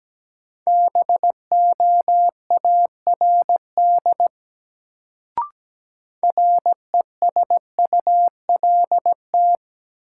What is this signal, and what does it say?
Morse code, heard as a keyed tone, about 17 wpm, 700 Hz. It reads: BOARD RESULT
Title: Sentences from Top 700 Words - New Word Drill - 17wpm